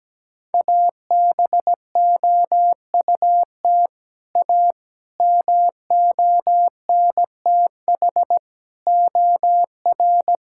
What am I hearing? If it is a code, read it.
Morse code, heard as a keyed tone, about 17 wpm, 700 Hz. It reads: ABOUT A MONTH OR
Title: Sentences from Top 1000 Words - Review Part 5 - 17wpm